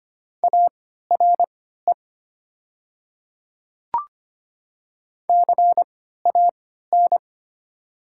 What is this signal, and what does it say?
Morse code, heard as a keyed tone, about 25 wpm, spaced as for 14 wpm, 690 Hz. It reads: ARE CAN
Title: 3x Character Spacing - Top 500 Words - 25wpm